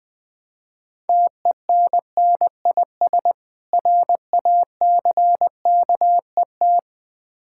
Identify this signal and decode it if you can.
Morse code, heard as a keyed tone, about 20 wpm, 690 Hz. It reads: TENNIS RACKET
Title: Difficult Words with Hints - 20wpm